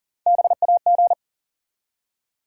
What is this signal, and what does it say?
Morse code, heard as a keyed tone, about 40 wpm, 700 Hz. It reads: BAG
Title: Three Letter Words - 40wpm